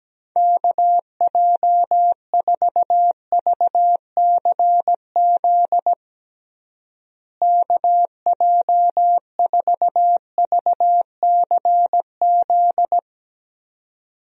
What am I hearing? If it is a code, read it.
Morse code, heard as a keyed tone, about 17 wpm, 700 Hz. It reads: KJ4VCZ KJ4VCZ
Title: USA Call Signs - 6 Characters - Repeated 2x - 17wpm